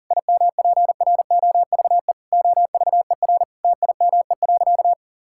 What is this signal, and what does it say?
Morse code, heard as a keyed tone, about 40 wpm, 690 Hz. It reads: IMPROVE OVER TIME.
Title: Sentences from Top 1000 Words - Review Part 5 - 40wpm